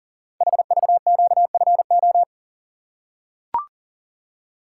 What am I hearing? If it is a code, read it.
Morse code, heard as a keyed tone, about 43 wpm, 700 Hz. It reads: HVQFO
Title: Code Groups - 5 Letters - 40wpm